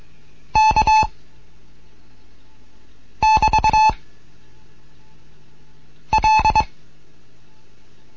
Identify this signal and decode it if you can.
Morse code, heard as a keyed tone, about 22 wpm, 870 Hz. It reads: K <BT> L